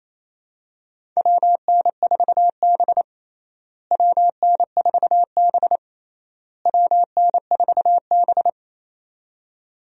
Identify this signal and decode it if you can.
Morse code, heard as a keyed tone, about 28 wpm, 700 Hz. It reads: WN4B WN4B WN4B
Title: USA Call Signs - 4 Characters - Repeated 3x - 28wpm